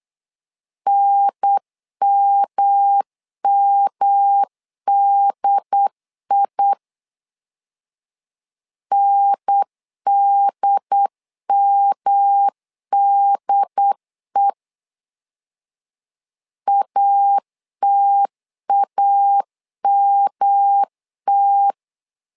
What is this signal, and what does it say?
Morse code, heard as a keyed tone, about 8 wpm, 800 Hz. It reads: NMMDI NDMDE ATAMT
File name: LEZIONI / 3 MWRO / 3 M 8wpm 2300